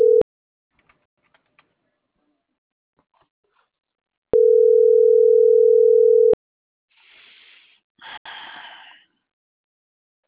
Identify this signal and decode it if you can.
Ringback tone (North America), 440 + 480 Hz.